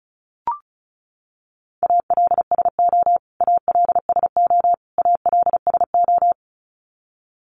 Morse code, heard as a keyed tone, about 34 wpm, 700 Hz. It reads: ALSO ALSO ALSO
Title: Top 200 Words Repeated 35wpm